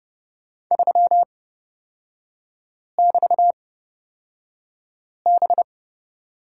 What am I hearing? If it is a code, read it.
Morse code, heard as a keyed tone, about 30 wpm, spaced as for 5 wpm, 700 Hz. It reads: 3<BT>B
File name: Lesson 36e - BT - Three Characters 30wpm